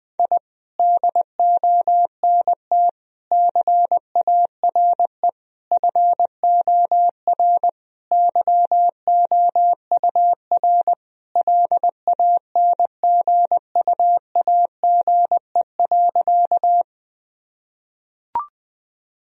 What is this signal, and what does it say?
Morse code, heard as a keyed tone, about 21 wpm, 700 Hz. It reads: I DONT CARE FOR YOUR LANGUAGE.
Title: Sentences from Top 1000 Words - Review Part 3 - 20wpm